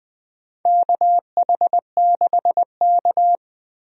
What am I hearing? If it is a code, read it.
Morse code, heard as a keyed tone, about 20 wpm, 700 Hz. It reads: KH6K
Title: DXCC-100 Call Signs - 4 Characters - Repeated 3x - 20wpm